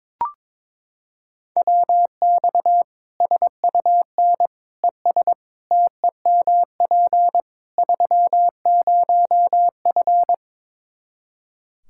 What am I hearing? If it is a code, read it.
Morse code, heard as a keyed tone, about 22 wpm, 700 Hz. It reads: WX SUN ES TEMP 30F
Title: Practice QSO - Exchange 2 - Info - 22wpm